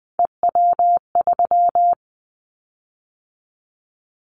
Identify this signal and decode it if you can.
Morse code, heard as a keyed tone, about 20 wpm, 700 Hz. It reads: EW3
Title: Callsign Prefixes for Top 100 DXCC - 20wpm